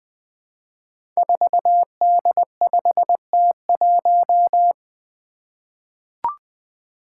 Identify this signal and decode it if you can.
Morse code, heard as a keyed tone, about 20 wpm, 700 Hz. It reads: 4D5T1
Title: Five Challenging Letter-Number Combinations 20wpm